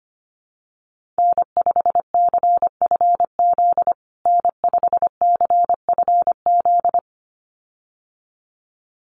Morse code, heard as a keyed tone, about 25 wpm, 700 Hz. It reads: N5CFZ N5CFZ
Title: USA Call Signs Repeated 2x 25wpm